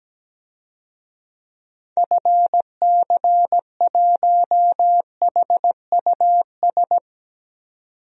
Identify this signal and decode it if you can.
Morse code, heard as a keyed tone, about 17 wpm, 700 Hz. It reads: FC1HUS
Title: DXCC-100 Call Signs - 6 Characters - 17wpm